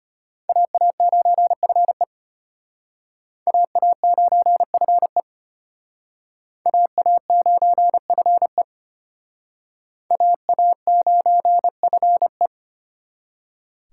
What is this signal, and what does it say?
Morse code, heard as a keyed tone, about 33 wpm, 700 Hz. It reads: AA9FE AA9FE AA9FE AA9FE